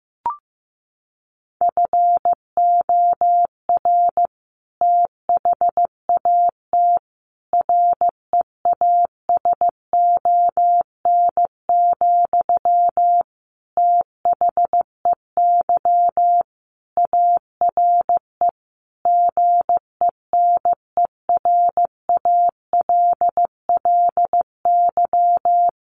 Morse code, heard as a keyed tone, about 15 wpm, 700 Hz. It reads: FOR THAT REASON, THEY ARE GENERALLY
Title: Sentences from Top 1500 Words 15wpm